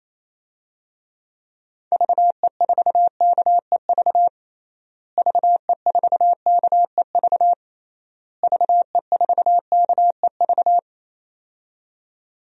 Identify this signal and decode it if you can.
Morse code, heard as a keyed tone, about 28 wpm, 700 Hz. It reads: VE4KEV VE4KEV VE4KEV
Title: DXCC-100 Call Signs - 6 Characters - Repeated 3x - 28wpm